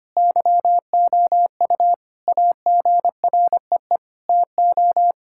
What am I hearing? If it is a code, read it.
Morse code, heard as a keyed tone, about 25 wpm, 700 Hz. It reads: YOU AGREE TO
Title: Sentences from Top 1000 Words - Review Part 6 - 25wpm